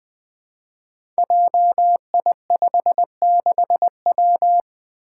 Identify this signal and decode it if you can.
Morse code, heard as a keyed tone, about 20 wpm, 700 Hz. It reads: JI56W
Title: Five Challenging Letter-Number Combinations 20wpm